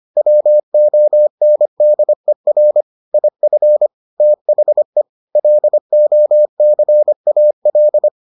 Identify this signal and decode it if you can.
Morse code, heard as a keyed tone, about 25 wpm, 600 Hz. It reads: WONDER IF THE LOCAL